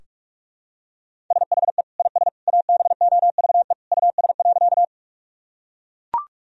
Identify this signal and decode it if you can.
Morse code, heard as a keyed tone, about 42 wpm, 700 Hz. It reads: SHE IS ABOVE US.
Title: Sentences from Top 1000 Words - Review Part 2 - 45wpm